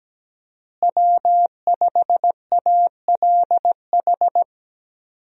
Morse code, heard as a keyed tone, about 17 wpm, 700 Hz. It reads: W5ALH
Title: USA Call Signs - 5 Characters - Repeated 3x - 17wpm